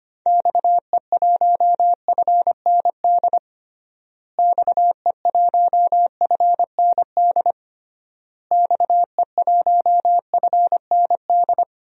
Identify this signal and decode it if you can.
Morse code, heard as a keyed tone, about 25 wpm, 700 Hz. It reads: XE1FND XE1FND XE1FND
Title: DXCC-100 Call Signs - 6 Characters - Repeated 3x - 25wpm